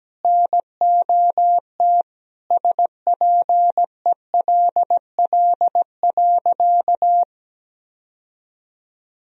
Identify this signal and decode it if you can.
Morse code, heard as a keyed tone, about 17 wpm, 700 Hz. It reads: NOT SPELL.
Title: Sentences from Top 1000 Words - Review Part 1 - 17wpm